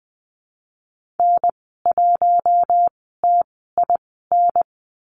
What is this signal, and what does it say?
Morse code, heard as a keyed tone, about 20 wpm, spaced as for 14 wpm, 700 Hz. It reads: N1TIN
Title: 042 Lesson 8g - Number 1 - Call Signs 20wpm